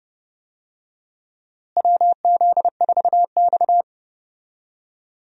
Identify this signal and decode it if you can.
Morse code, heard as a keyed tone, about 30 wpm, 700 Hz. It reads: WZ4X
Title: USA Call Signs - 4 Characters - Repeated 3x - 30wpm